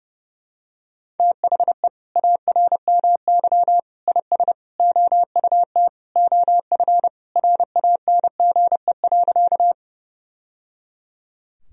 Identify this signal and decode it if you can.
Morse code, heard as a keyed tone, about 30 wpm, 700 Hz. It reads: THE ARMY IS OUT OF RANGE.
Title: Sentences from Top 1000 Words - Review Part 4 - 30wpm